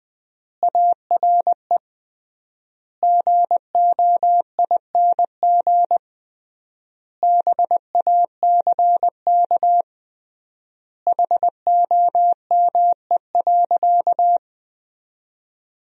Morse code, heard as a keyed tone, about 20 wpm, 700 Hz. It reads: ARE GOING BACK HOME.
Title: 3x Word-Spacing - Sentences from Top 500 Words - 20wpm